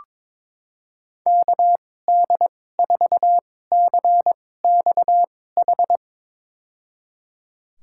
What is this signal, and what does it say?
Morse code, heard as a keyed tone, about 21 wpm, spaced as for 16 wpm, 700 Hz. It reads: KD4CXH
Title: USA Call Signs - 6 Characters - 2x Character Spacing - 22wpm